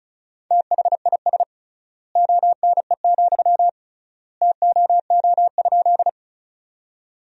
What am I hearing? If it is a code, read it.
Morse code, heard as a keyed tone, about 36 wpm, 700 Hz. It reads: THIS ONE, TOO?
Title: 3x Word-Spacing - Sentences from Top 200 Words - 35wpm